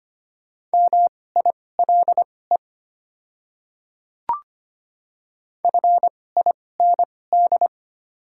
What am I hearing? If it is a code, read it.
Morse code, heard as a keyed tone, about 25 wpm, spaced as for 18 wpm, 700 Hz. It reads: MILE FIND
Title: 2x Character Spacing - Top 300 Words - 25wpm